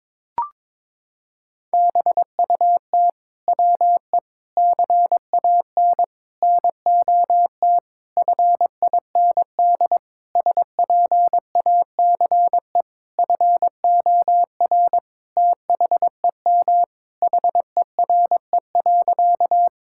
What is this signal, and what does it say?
Morse code, heard as a keyed tone, about 22 wpm, 700 Hz. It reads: BUT WE CAN NOT FIND SPACE FOR THEM HERE.
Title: Sentences from Top 1000 Words - Review Part 5 - 22wpm